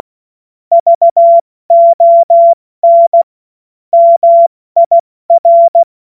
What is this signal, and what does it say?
Morse code, heard as a keyed tone, about 16 wpm, 680 Hz. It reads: VON MIR
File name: Morsekurs / Witze / Witz 13. Juni 2023 70 F80